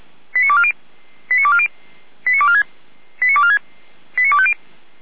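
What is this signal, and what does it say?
Five-tone selective call (ZVEI), heard as quick runs of five tones, tones of about 70 ms. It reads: 89240 89240 89246 89246 89250
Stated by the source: ZVEI five-tone call received on SDR, feuerwehr-orlamuende.org (fire brigade Orlamünde).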